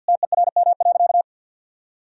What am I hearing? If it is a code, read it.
Morse code, heard as a keyed tone, about 50 wpm, 700 Hz. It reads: TERM.
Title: Sentences from Top 1000 Words - Review Part 4 - 50wpm